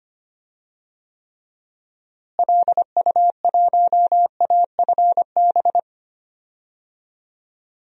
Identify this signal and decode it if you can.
Morse code, heard as a keyed tone, about 25 wpm, 700 Hz. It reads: LU1AFB
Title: DXCC-100 Call Signs - 6 Characters - Repeated 3x - 25wpm